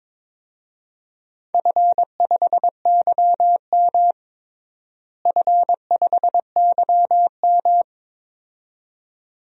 Morse code, heard as a keyed tone, about 22 wpm, 700 Hz. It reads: F5YM F5YM
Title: DXCC-100 Call Signs - 4 Characters - Repeated 2x - 22wpm